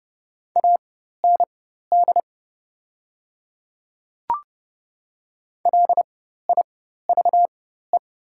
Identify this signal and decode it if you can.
Morse code, heard as a keyed tone, about 30 wpm, spaced as for 14 wpm, 700 Hz. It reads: AND LIVE
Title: 4x Character Spacing - Top 300 Words - 30wpm